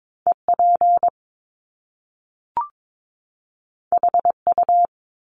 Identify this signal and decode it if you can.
Morse code, heard as a keyed tone, about 22 wpm, 700 Hz. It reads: EP HU
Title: Top Two Letter Combinations in Words 22wpm